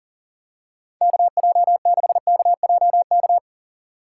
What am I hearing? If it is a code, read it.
Morse code, heard as a keyed tone, about 40 wpm, 700 Hz. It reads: KJ6KJK
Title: USA Call Signs - 6 Characters - 6x Character Spacing - 40wpm